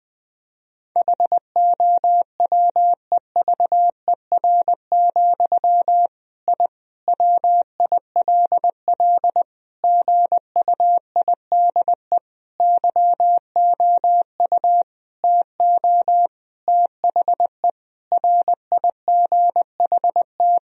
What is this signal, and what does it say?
Morse code, heard as a keyed tone, about 21 wpm, 700 Hz. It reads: HOWEVER, I WILL GUIDE YOU TO THE RIGHT
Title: Sentences from Top 1000 Words - Review Part 3 - 20wpm